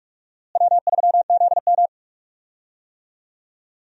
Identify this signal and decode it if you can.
Morse code, heard as a keyed tone, about 46 wpm, 700 Hz. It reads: W3ZM
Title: USA Call Signs - 4 Characters - Repeated 3x - 45wpm